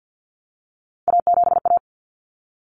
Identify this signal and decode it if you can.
Morse code, heard as a keyed tone, about 48 wpm, 700 Hz. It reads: A7A
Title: Common Cut Numbers - 3-Digit 50wpm